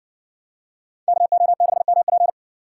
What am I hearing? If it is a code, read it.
Morse code, heard as a keyed tone, about 56 wpm, 700 Hz. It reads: KO6MP